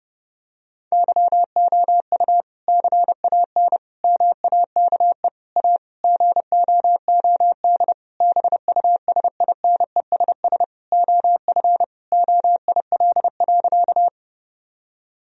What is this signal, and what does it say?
Morse code, heard as a keyed tone, about 32 wpm, 700 Hz. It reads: YOU CAN MAKE A GOOD BUSINESS OF OIL.
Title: Sentences from Top 1000 Words - Review Part 4 - 30wpm